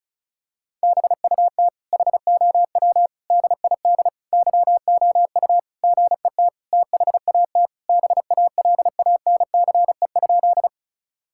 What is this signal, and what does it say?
Morse code, heard as a keyed tone, about 36 wpm, 700 Hz. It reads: BUT HOW DID YOU GET THAT BALANCE?
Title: Sentences from Top 1000 Words - Review Part 6 - 35wpm